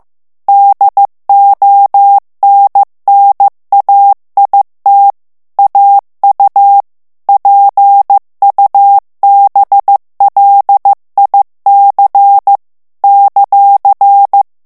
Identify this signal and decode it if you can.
Morse code, heard as a keyed tone, about 15 wpm, 800 Hz. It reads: DONNAIT AU PUBLIC ;
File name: morse mp3 / FRANCAIS TEXTE CLAIR 2 VITESSE 14 MOTS MINUTES